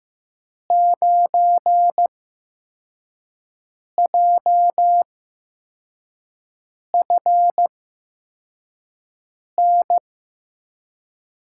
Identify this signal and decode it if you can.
Morse code, heard as a keyed tone, about 15 wpm, spaced as for 4 wpm, 700 Hz. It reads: 9JFN
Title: USA Call Signs - 6 Characters - 8x Character Spacing - 15wpm